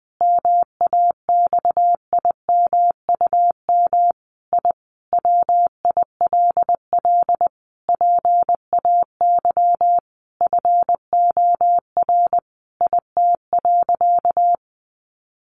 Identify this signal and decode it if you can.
Morse code, heard as a keyed tone, about 20 wpm, 700 Hz. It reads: MAXIMUM I WILL PAY FOR IT.
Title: Sentences from Top 1300 Words 20wpm